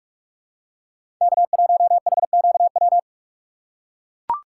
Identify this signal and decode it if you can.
Morse code, heard as a keyed tone, about 41 wpm, 700 Hz. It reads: K1HQW
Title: USA Call Signs - 5 Characters - 6x Character Spacing - 45wpm